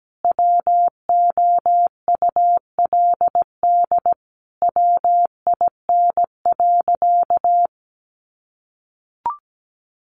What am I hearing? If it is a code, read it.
Morse code, heard as a keyed tone, about 17 wpm, 700 Hz. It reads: WOULD WIN.
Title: Sentences from Top 1500 Words 17wpm